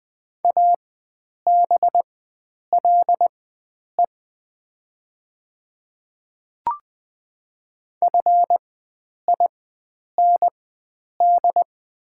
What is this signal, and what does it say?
Morse code, heard as a keyed tone, about 20 wpm, spaced as for 9 wpm, 700 Hz. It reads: ABLE FIND
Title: 4x Character Spacing - Top 500 Words - 20wpm